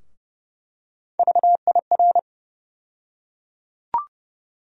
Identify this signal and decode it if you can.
Morse code, heard as a keyed tone, about 30 wpm, 700 Hz. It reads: VIR
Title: USA Call Sign Postfixes - 3 Letters 30wpm